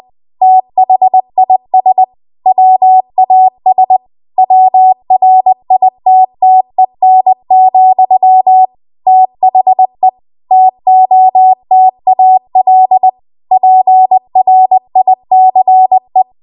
Morse code, heard as a keyed tone, about 20 wpm, 750 Hz. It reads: THIS WAS WRITTEN, THE TOTAL PRICE
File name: Archive / 20 WPM / 20WPM